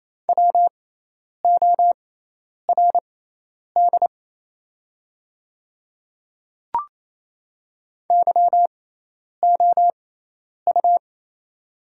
Morse code, heard as a keyed tone, about 28 wpm, spaced as for 10 wpm, 700 Hz. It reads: WORD YOU
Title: 6x Character Spacing - Top 100 Words - 28wpm